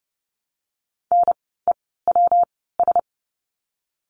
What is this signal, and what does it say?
Morse code, heard as a keyed tone, about 30 wpm, 700 Hz. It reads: N E W S
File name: course / 30:10wpm / 115 Lesson 19f - Letter w - Words 30wpm